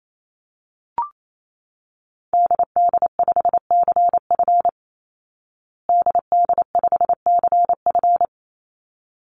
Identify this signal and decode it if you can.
Morse code, heard as a keyed tone, about 28 wpm, 700 Hz. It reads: DD5CF DD5CF
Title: Call Signs Repeated 3x - 28wpm